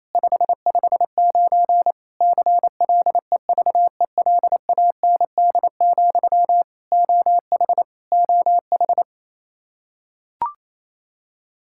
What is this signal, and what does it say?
Morse code, heard as a keyed tone, about 28 wpm, 700 Hz. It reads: 559 CLEVELAND, OH OH
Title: Practice QSO - Exchange 1 - Beginning - 28wpm